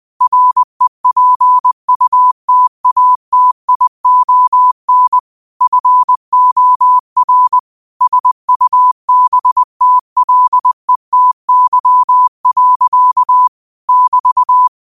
Morse code, heard as a keyed tone, about 20 wpm, 1000 Hz. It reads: REPUTATION FOR SUBTLETY. <BT>
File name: qotdfiles / 20wpm